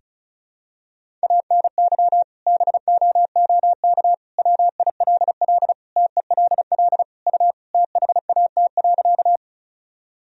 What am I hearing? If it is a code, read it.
Morse code, heard as a keyed tone, about 36 wpm, 700 Hz. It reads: ANY BOOK WILL TELL U THAT.
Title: Sentences from Top 300 Words and Abbreviations 35wpm